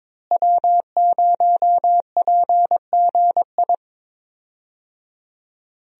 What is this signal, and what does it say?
Morse code, heard as a keyed tone, about 22 wpm, 700 Hz. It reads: W0PGI
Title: USA Call Signs - 5 Characters - Repeated 3x - 22wpm